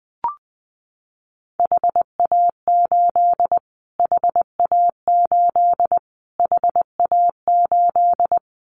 Morse code, heard as a keyed tone, about 20 wpm, 700 Hz. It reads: HA8 HA8 HA8
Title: Callsign Prefixes for Top 25 DXCC - Repeated 3x - 20wpm